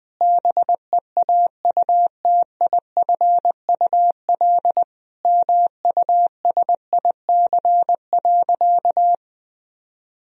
Morse code, heard as a keyed tone, about 20 wpm, 700 Hz. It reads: BEAUTIFUL MUSIC.
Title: Sentences from Top 1000 Words - Review Part 1 - 20wpm